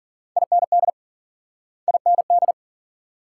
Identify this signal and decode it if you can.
Morse code, heard as a keyed tone, about 50 wpm, 700 Hz. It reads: IND IND